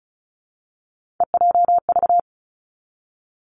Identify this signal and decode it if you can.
Morse code, heard as a keyed tone, about 36 wpm, 700 Hz. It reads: EJV
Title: Three Challenging Letter-Number Combinations 35wpm